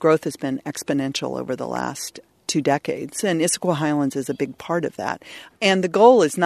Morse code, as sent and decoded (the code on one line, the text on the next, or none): none